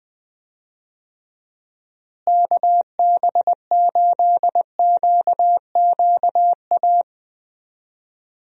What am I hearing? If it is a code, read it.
Morse code, heard as a keyed tone, about 21 wpm, 700 Hz. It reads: KB8QQA
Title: USA Call Signs - 6 Characters - Repeated 2x - 20wpm